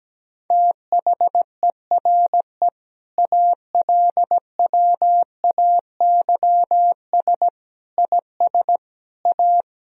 Morse code, heard as a keyed tone, about 17 wpm, 700 Hz. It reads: THERE ALWAYS IS A
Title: Sentences from Top 1000 Words - Review Part 2 - 17wpm